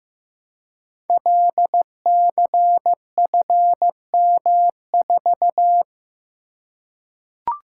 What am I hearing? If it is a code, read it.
Morse code, heard as a keyed tone, about 15 wpm, 700 Hz. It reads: LCFM4
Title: Code Groups - 5 Letter:Numbers - 15wpm